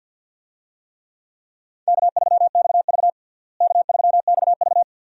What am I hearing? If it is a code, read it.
Morse code, heard as a keyed tone, about 53 wpm, 700 Hz. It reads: K3XV K3XV
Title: USA Call Signs - 4 Characters - Repeated 2x - 50wpm